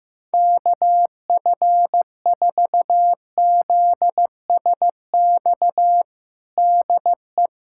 Morse code, heard as a keyed tone, about 15 wpm, 700 Hz. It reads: KF4ZSX DE
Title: USA Callsign DE Callsign 15wpm